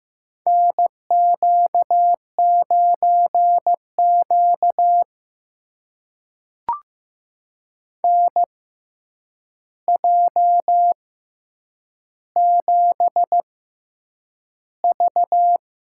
Morse code, heard as a keyed tone, about 15 wpm, spaced as for 8 wpm, 700 Hz. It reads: NQ9Q NJ7V
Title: USA Call Signs - 4 Characters - 6x Character Spacing - 15wpm